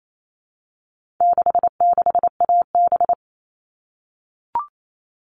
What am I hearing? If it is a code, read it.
Morse code, heard as a keyed tone, about 28 wpm, 700 Hz. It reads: 66AB
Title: All Cut Numbers - 4-Digit 28wpm